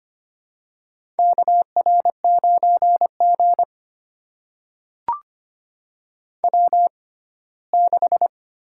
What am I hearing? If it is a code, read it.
Morse code, heard as a keyed tone, about 25 wpm, spaced as for 17 wpm, 700 Hz. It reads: KR9G W6
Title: USA Call Signs - 4 Characters - 6x Character Spacing - 25wpm